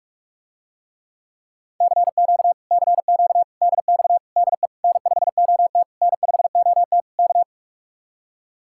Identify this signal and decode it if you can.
Morse code, heard as a keyed tone, about 49 wpm, 700 Hz. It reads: CQ CQ DX DE N5OT N5OT K